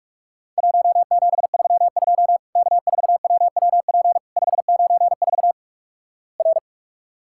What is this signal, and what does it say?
Morse code, heard as a keyed tone, about 46 wpm, 700 Hz. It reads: 1732 K4WWP 594 R
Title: Worked All Europe DX Contest - QTC Exchange - 45wpm